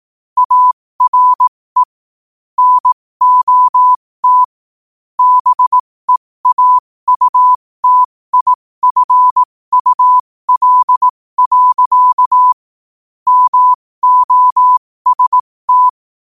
Morse code, heard as a keyed tone, about 18 wpm, spaced as for 15 wpm, 1000 Hz. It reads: ARE NOT BEAUTIFUL. MOST
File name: qotdfiles / 15wpm